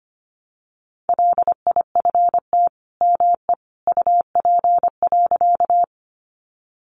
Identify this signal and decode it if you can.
Morse code, heard as a keyed tone, about 25 wpm, 700 Hz. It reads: LIFT ME UP.
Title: Sentences from Top 1000 Words 25wpm